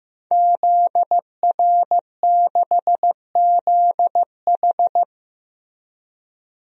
Morse code, heard as a keyed tone, about 15 wpm, 700 Hz. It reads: ZR6ZH
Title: DXCC-100 Call Signs - 5 Characters - Repeated 3x - 15wpm